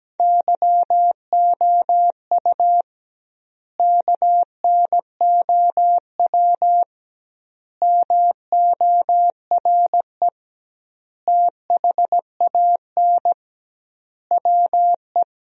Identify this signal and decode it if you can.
Morse code, heard as a keyed tone, about 17 wpm, 700 Hz. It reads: YOU KNOW MORE THAN WE
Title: 2x Word-Spacing - Sentences from Top 100 Words - 17wpm